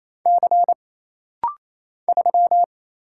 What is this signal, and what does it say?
Morse code, heard as a keyed tone, about 28 wpm, 700 Hz. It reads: C 3
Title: Single Letter-Number Alt - Rapid-Fire 28wpm